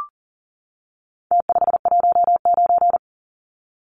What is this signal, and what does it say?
Morse code, heard as a keyed tone, about 40 wpm, 700 Hz. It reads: T519
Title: Numbers - Contest Style - 4-Digit 40wpm